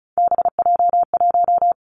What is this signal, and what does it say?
Morse code, heard as a keyed tone, about 36 wpm, 700 Hz. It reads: BJ1
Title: Three Challenging Letter-Number Combinations 35wpm